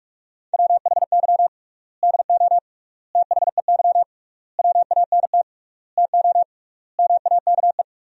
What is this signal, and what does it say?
Morse code, heard as a keyed tone, about 46 wpm, 700 Hz. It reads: WHY DO THEY WANT TO MAKE
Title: 3x Word-Spacing - Sentences from Top 400 Words - 45wpm